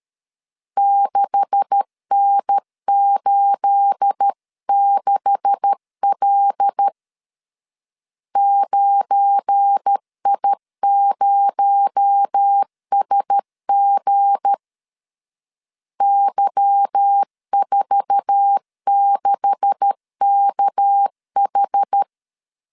Morse code, heard as a keyed tone, about 13 wpm, 800 Hz. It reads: 6N86L 9I0SG Y46KH
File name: CW / DET / 1 9 1500C S A 12wpm 3800